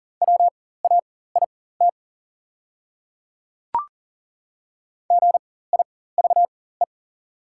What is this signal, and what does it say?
Morse code, heard as a keyed tone, about 40 wpm, spaced as for 19 wpm, 700 Hz. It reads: WAIT GIVE
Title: 4x Character Spacing - Top 500 Words - 40wpm